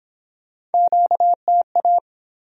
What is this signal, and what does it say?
Morse code, heard as a keyed tone, about 30 wpm, 700 Hz. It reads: QTA